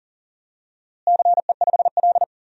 Morse code, heard as a keyed tone, about 40 wpm, 700 Hz. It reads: CE5L
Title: DXCC-100 Call Signs - 4 Characters - Repeated 3x - 40wpm